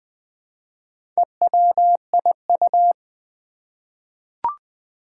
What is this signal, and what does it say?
Morse code, heard as a keyed tone, about 20 wpm, 700 Hz. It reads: EWIU
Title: Four Challenging Letter-Number Combinations 20wpm